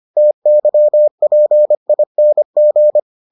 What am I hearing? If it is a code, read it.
Morse code, heard as a keyed tone, about 25 wpm, 600 Hz. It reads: TYPING